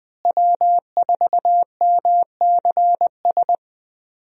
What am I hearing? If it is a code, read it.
Morse code, heard as a keyed tone, about 20 wpm, 700 Hz. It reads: W4MCS
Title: USA Call Signs - 5 Characters - Repeated 3x - 20wpm